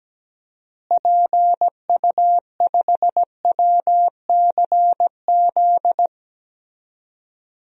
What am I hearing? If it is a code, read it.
Morse code, heard as a keyed tone, about 17 wpm, 700 Hz. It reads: PU5WCZ
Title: DXCC-100 Call Signs - 6 Characters - Repeated 3x - 17wpm